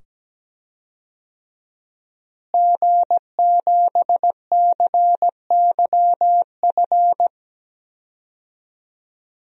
Morse code, heard as a keyed tone, about 17 wpm, 700 Hz. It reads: G7CYF
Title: DXCC-100 Call Signs - 5 Characters - Repeated 2x - 17wpm